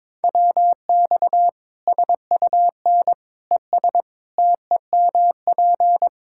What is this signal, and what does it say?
Morse code, heard as a keyed tone, about 21 wpm, 700 Hz. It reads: WX SUN ES TEMP
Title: Practice QSO - Exchange 2 - Info - 22wpm